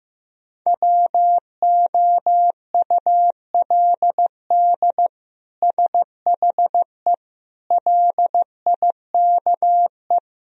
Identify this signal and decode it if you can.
Morse code, heard as a keyed tone, about 16 wpm, 700 Hz. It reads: WOULD SHE LIKE
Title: Sentences from Top 1000 Words - Review Part 2 - 15wpm